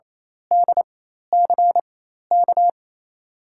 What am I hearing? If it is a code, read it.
Morse code, heard as a keyed tone, about 28 wpm, spaced as for 13 wpm, 700 Hz. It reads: DCK
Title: USA Call Signs - 4x Character Spacing - 28wpm